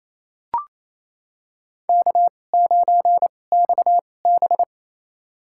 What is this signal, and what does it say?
Morse code, heard as a keyed tone, about 28 wpm, spaced as for 20 wpm, 700 Hz. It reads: K9XB
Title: USA Call Signs - 4 Characters - 2x Character Spacing - 28wpm